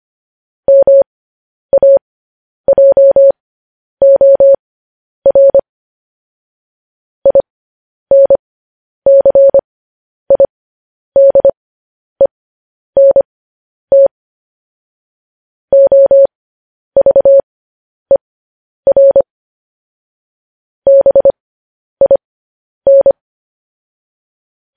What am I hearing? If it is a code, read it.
Morse code, heard as a keyed tone, about 25 wpm, spaced as for 10 wpm, 550 Hz. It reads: MAJOR INCIDENT OVER BIN
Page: news-headlines-in-morse-code-at-10-wpm